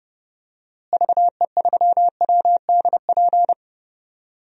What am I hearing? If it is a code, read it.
Morse code, heard as a keyed tone, about 30 wpm, 700 Hz. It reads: VE3WDP
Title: DXCC-100 Call Signs - 6 Characters - Repeated 3x - 30wpm